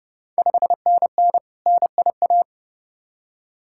Morse code, heard as a keyed tone, about 30 wpm, 690 Hz. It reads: 5NN NIA
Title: Ontario QSO Party Exchange 30wpm